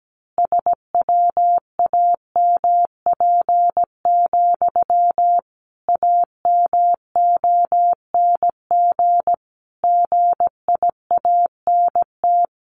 Morse code, heard as a keyed tone, about 17 wpm, 700 Hz. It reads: SWAMP, AMONG GIANT
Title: A Sound Of Thunder 17wpm